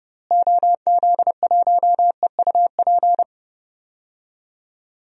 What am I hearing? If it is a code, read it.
Morse code, heard as a keyed tone, about 31 wpm, 700 Hz. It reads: OZ1EUP
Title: DXCC-100 Call Signs - 6 Characters - Repeated 2x - 30wpm